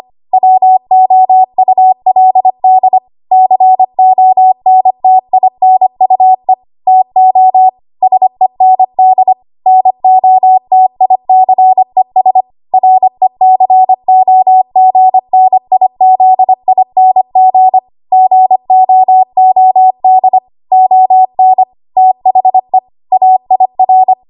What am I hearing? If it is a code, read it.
Morse code, heard as a keyed tone, about 25 wpm, 750 Hz. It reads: WOULD CONTINUE TO SEND NOTICES RECOGNIZING GOOD ON THE AIR
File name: Archive / 25 WPM / 25WPM